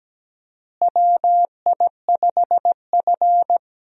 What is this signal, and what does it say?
Morse code, heard as a keyed tone, about 17 wpm, 700 Hz. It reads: WI5F
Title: USA Call Signs - 4 Characters - Repeated 3x - 17wpm